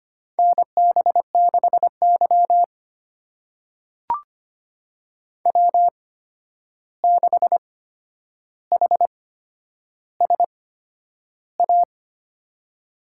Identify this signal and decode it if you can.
Morse code, heard as a keyed tone, about 25 wpm, spaced as for 10 wpm, 700 Hz. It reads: NB6Y W6HSA
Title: USA Call Signs - 8x Character Spacing - 25wpm